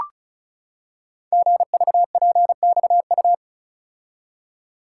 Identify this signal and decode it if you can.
Morse code, heard as a keyed tone, about 36 wpm, 700 Hz. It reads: GVPXU
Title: Code Groups - 5 Letters - 35wpm